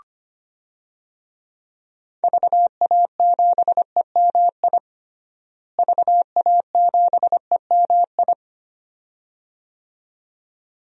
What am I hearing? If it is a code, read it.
Morse code, heard as a keyed tone, about 25 wpm, 700 Hz. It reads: VA7EMI VA7EMI
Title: DXCC-100 Call Signs - 6 Characters - Repeated 2x - 25wpm